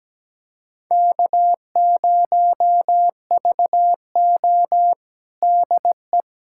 Morse code, heard as a keyed tone, about 17 wpm, 700 Hz. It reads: K0VO DE
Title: USA Callsign DE Callsign - 4 Characters - 17wpm